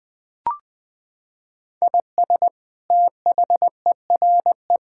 Morse code, heard as a keyed tone, about 20 wpm, 700 Hz. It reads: IS THERE
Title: Sentences from Top 1000 Words - Review Part 6 - 20wpm